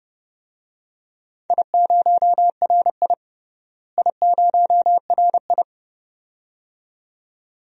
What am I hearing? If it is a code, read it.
Morse code, heard as a keyed tone, about 30 wpm, 700 Hz. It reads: I0RI I0RI
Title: DXCC-100 Call Signs - 4 Characters - Repeated 2x - 30wpm